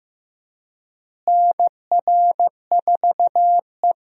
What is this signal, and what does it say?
Morse code, heard as a keyed tone, about 15 wpm, 700 Hz. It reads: NR4E
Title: USA Call Signs - 4 Characters - Repeated 3x - 15wpm